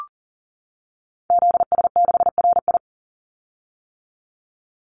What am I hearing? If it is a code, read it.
Morse code, heard as a keyed tone, about 40 wpm, 700 Hz. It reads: ZS6RI
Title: Call Signs - 40wpm